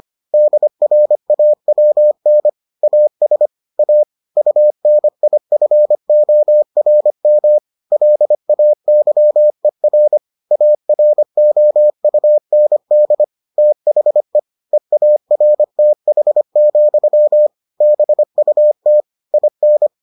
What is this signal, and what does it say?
Morse code, heard as a keyed tone, about 25 wpm, 600 Hz. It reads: DRAWN AS A UNIFORM LAYER AROUND THE EARTH, BUT IN